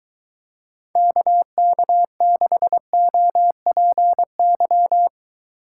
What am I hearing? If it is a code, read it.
Morse code, heard as a keyed tone, about 23 wpm, 700 Hz. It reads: KK6OPY